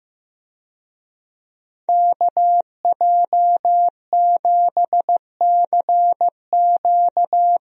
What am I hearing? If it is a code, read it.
Morse code, heard as a keyed tone, about 16 wpm, 700 Hz. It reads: KJ7CQ
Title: USA Call Signs - 5 Characters - Repeated 3x - 15wpm